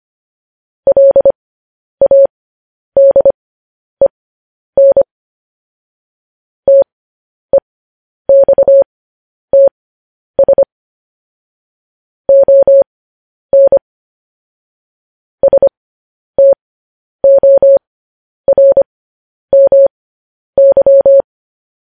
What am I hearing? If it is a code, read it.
Morse code, heard as a keyed tone, about 25 wpm, spaced as for 10 wpm, 550 Hz. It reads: LADEN TEXTS ON STORMY